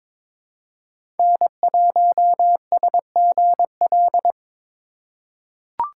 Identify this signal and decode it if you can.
Morse code, heard as a keyed tone, about 22 wpm, 700 Hz. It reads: N1SGL
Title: USA Call Signs - 5 Characters - 4x Character Spacing - 22wpm